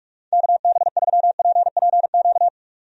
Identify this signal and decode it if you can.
Morse code, heard as a keyed tone, about 46 wpm, 700 Hz. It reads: KB3PPQ
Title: USA Call Signs - 6 Characters - 45wpm